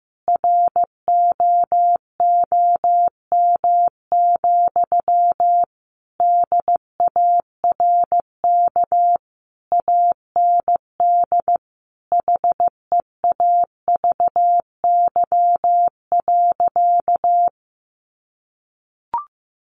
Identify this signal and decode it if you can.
Morse code, heard as a keyed tone, about 15 wpm, 700 Hz. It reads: ROOM, DARK AND HEAVY.
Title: Sentences from Top 1400 Words 15wpm